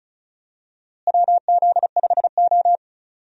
Morse code, heard as a keyed tone, about 36 wpm, 700 Hz. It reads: WZ5O
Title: USA Call Signs - 4 Characters - Repeated 3x - 35wpm